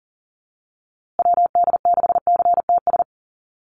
Morse code, heard as a keyed tone, about 40 wpm, 700 Hz. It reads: WD6CTS
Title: USA Call Signs Repeated 3x 40wpm